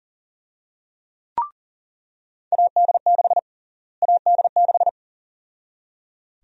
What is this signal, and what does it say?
Morse code, heard as a keyed tone, about 37 wpm, 700 Hz. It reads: AD6 AD6
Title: USA Call Sign Prefixes - Repeated 2x - 40wpm